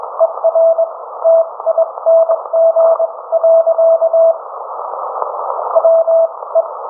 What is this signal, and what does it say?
Morse code, heard as a keyed tone, about 21 wpm, spaced as for 18 wpm, 660 Hz. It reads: ER TING. WE